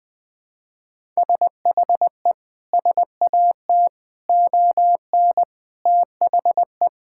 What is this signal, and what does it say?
Morse code, heard as a keyed tone, about 19 wpm, 700 Hz. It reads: SHE SAT ON THE
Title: Sentences with 6-Word Phrases - Spoken Beginning - 20wpm